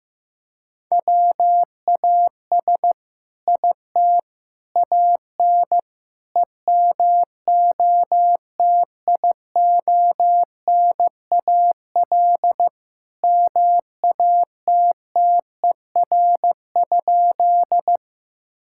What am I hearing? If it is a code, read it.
Morse code, heard as a keyed tone, about 15 wpm, 700 Hz. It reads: WAS IT AN EMOTIONAL MATTER?
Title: Sentences from Top 1000 Words - Review Part 5 - 15wpm